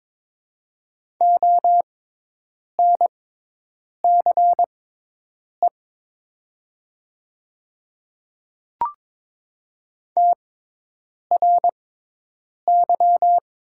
Morse code, heard as a keyed tone, about 22 wpm, spaced as for 8 wpm, 700 Hz. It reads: ONCE TRY
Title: 6x Character Spacing - Top 300 Words - 22wpm